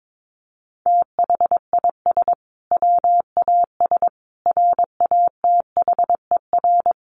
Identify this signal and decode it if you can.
Morse code, heard as a keyed tone, about 22 wpm, 700 Hz. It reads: THIS WAS RATHER
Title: Sentences from Top 1400 Words 22wpm